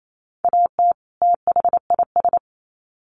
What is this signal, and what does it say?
Morse code, heard as a keyed tone, about 28 wpm, 700 Hz. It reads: AT THIS
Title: Sets of 2 Words 28wpm